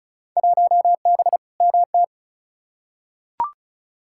Morse code, heard as a keyed tone, about 36 wpm, 700 Hz. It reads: 1B MT
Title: Field Day Exchanges - 35wpm